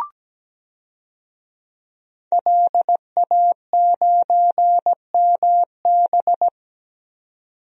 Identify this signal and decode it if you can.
Morse code, heard as a keyed tone, about 17 wpm, 700 Hz. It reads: LA9MB
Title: DXCC-100 Call Signs - 5 Characters - 17wpm